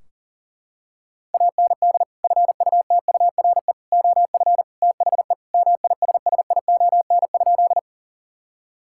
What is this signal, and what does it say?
Morse code, heard as a keyed tone, about 40 wpm, 700 Hz. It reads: AND FUTURE OF THE MISSION?
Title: Sentences from Top 1600 Words 40wpm